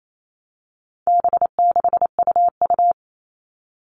Morse code, heard as a keyed tone, about 28 wpm, 700 Hz. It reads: B6UU
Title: All Cut Numbers - 4-Digit 28wpm